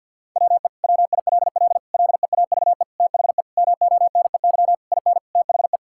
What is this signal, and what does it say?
Morse code, heard as a keyed tone, about 52 wpm, 700 Hz. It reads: WE WILL LEAVE THE MONEY IN THE
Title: Sentences from Top 1000 Words - Review Part 6 - 50wpm